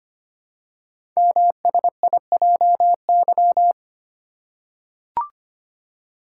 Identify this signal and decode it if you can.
Morse code, heard as a keyed tone, about 25 wpm, 700 Hz. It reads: MSIJY
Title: Code Groups - 5 Letters - 25wpm